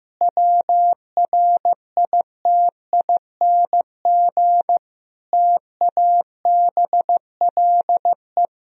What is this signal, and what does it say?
Morse code, heard as a keyed tone, about 15 wpm, 700 Hz. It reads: WRITING TABLE
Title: Sentences from Top 1000 Words - Review Part 3 - 15wpm